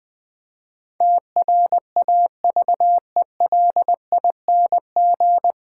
Morse code, heard as a keyed tone, about 20 wpm, 700 Hz. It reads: TRAVELING
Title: Difficult Sentences with Hints - 20wpm